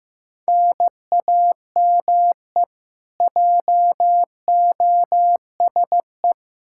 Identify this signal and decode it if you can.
Morse code, heard as a keyed tone, about 15 wpm, 700 Hz. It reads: NAME JOSE
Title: Practice QSO - Exchange 1 - Beginning - 15wpm